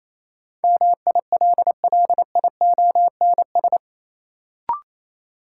Morse code, heard as a keyed tone, about 28 wpm, 700 Hz. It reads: MILLIONS
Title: Sets of 5 Words - New Word Drill - 28wpm